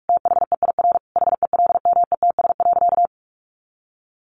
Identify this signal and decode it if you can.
Morse code, heard as a keyed tone, about 46 wpm, 700 Hz. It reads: THEIR HELMETS.
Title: A Sound Of Thunder 45wpm